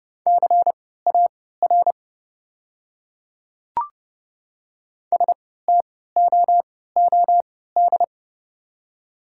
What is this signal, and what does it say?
Morse code, heard as a keyed tone, about 30 wpm, spaced as for 17 wpm, 690 Hz. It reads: CAR STOOD
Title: 3x Character Spacing - Top 500 Words - 30wpm